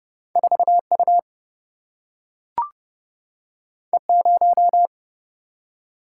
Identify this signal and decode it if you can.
Morse code, heard as a keyed tone, about 30 wpm, 700 Hz. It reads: 4U E0
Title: Single-Letter Single-Number 30wpm